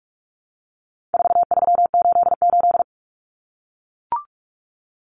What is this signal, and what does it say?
Morse code, heard as a keyed tone, about 44 wpm, 700 Hz. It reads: 4388
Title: Common Cut Numbers - 4-Digit 45wpm